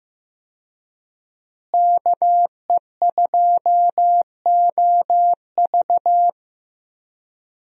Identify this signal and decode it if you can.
Morse code, heard as a keyed tone, about 15 wpm, 700 Hz. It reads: KE2OV
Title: USA Call Signs - 5 Characters - Repeated 3x - 15wpm